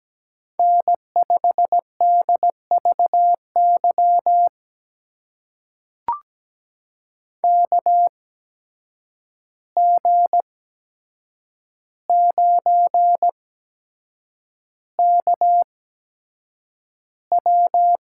Morse code, heard as a keyed tone, about 17 wpm, spaced as for 7 wpm, 700 Hz. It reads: N5DVY KG9KW
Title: USA Call Signs - 5 Characters - 8x Character Spacing - 17wpm